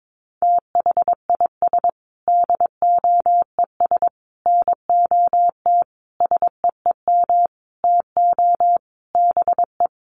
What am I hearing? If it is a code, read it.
Morse code, heard as a keyed tone, about 22 wpm, 700 Hz. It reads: THIS DOES NOT SEEM TO BE
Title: Sentences from Top 1400 Words 22wpm